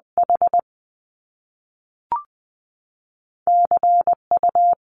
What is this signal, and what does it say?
Morse code, heard as a keyed tone, about 20 wpm, 700 Hz. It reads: H C U
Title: Top Two Letter Combinations in Words 20wpm